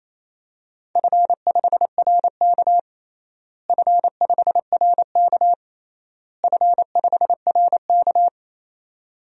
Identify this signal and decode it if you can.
Morse code, heard as a keyed tone, about 28 wpm, 700 Hz. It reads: F5RK F5RK F5RK
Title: DXCC-100 Call Signs - 4 Characters - Repeated 3x - 28wpm